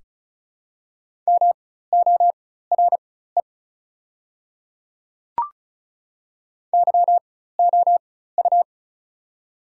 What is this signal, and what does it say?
Morse code, heard as a keyed tone, about 34 wpm, spaced as for 16 wpm, 700 Hz. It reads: MORE YOU
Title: 4x Character Spacing - Top 100 Words - 35wpm